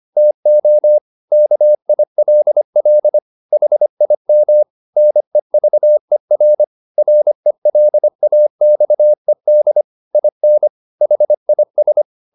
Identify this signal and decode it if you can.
Morse code, heard as a keyed tone, about 25 wpm, 600 Hz. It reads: TO KILL HIM NEVER RELAXED IN HIS